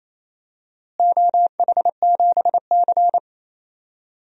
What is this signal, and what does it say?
Morse code, heard as a keyed tone, about 28 wpm, 700 Hz. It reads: OH7C
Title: DXCC-100 Call Signs - 4 Characters - Repeated 3x - 28wpm